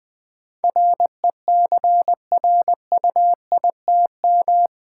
Morse code, heard as a keyed tone, about 20 wpm, 700 Hz. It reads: RECRUITM
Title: Build A Word - 4 Letters At A Time - 11-Letter Words - 20wpm